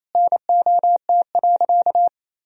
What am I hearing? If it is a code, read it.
Morse code, heard as a keyed tone, about 28 wpm, 700 Hz. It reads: NOT.
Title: Sentences from Top 1000 Words - Review Part 4 - 28wpm